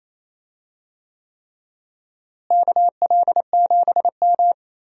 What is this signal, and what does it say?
Morse code, heard as a keyed tone, about 28 wpm, 700 Hz. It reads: KL7M
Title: DXCC-100 Call Signs - 4 Characters - Repeated 3x - 28wpm